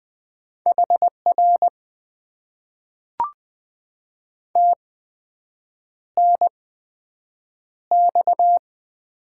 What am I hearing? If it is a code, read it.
Morse code, heard as a keyed tone, about 20 wpm, spaced as for 7 wpm, 700 Hz. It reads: HR TNX
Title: Top 90 QSO Elements - 8x Character Spacing - 20wpm